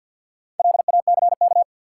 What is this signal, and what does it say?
Morse code, heard as a keyed tone, about 50 wpm, 700 Hz. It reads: LACK
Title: Sentences from Top 1100 Words - New Word Drill - 50wpm